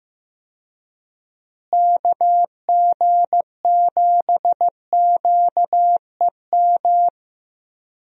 Morse code, heard as a keyed tone, about 15 wpm, 700 Hz. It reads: KG7QEM
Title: USA Call Signs - 6 Characters - Repeated 3x - 15wpm